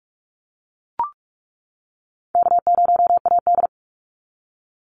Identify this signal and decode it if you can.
Morse code, heard as a keyed tone, about 45 wpm, 700 Hz. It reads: K0AD
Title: Easy Call Signs - 45wpm